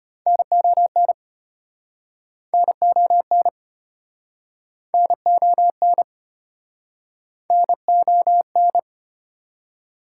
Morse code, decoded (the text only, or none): NON NON NON NON